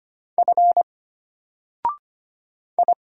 Morse code, heard as a keyed tone, about 25 wpm, 700 Hz. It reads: F I
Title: Single Character (Letter:Number:Punctuation:Prosign) Alt 25wpm